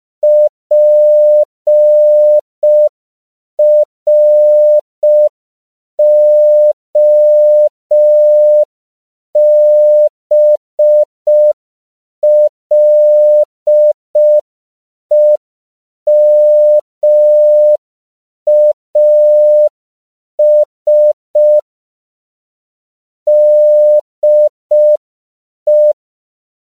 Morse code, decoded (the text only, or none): PROBLEMAS DE